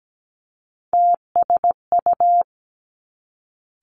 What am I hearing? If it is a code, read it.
Morse code, heard as a keyed tone, about 17 wpm, 700 Hz. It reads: TSU
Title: Three Challenging Letter-Number Combinations 17wpm